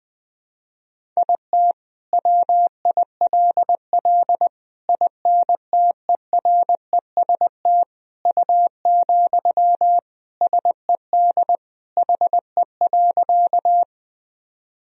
Morse code, heard as a keyed tone, about 20 wpm, 700 Hz. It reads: IT WILL INTEREST U, SED HE.
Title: Sentences from Top 400 Words and Abbreviations 20wpm